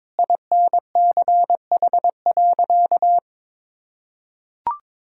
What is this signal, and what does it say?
Morse code, heard as a keyed tone, about 22 wpm, 700 Hz. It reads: INCH.
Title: Sentences from Top 1000 Words - Review Part 3 - 22wpm